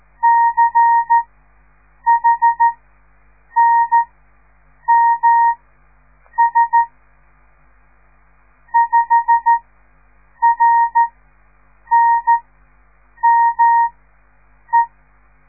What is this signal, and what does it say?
Morse code, heard as a keyed tone, about 13 wpm, spaced as for 7 wpm, 930 Hz. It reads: CHNMS 5RNME
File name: radiomorningstar / morse / les 6